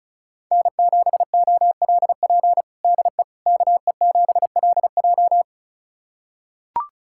Morse code, heard as a keyed tone, about 36 wpm, 700 Hz. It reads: N7OLP DE KE7LJ
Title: USA Callsign DE Callsign - 5 Characters - 35wpm